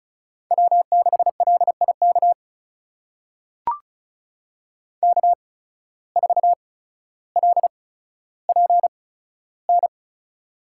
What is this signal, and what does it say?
Morse code, heard as a keyed tone, about 36 wpm, spaced as for 15 wpm, 700 Hz. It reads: W6LIK K4LPN
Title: USA Call Signs - 5 Characters - 8x Character Spacing - 35wpm